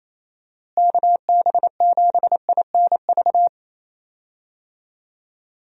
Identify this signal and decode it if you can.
Morse code, heard as a keyed tone, about 28 wpm, 700 Hz. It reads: KB7INV
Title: USA Call Signs - 6 Characters - Repeated 2x - 28wpm